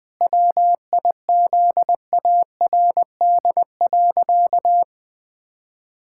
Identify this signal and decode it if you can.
Morse code, heard as a keyed tone, about 21 wpm, 700 Hz. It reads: WIZARD.
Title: Pangram Challenge v2 - 20wpm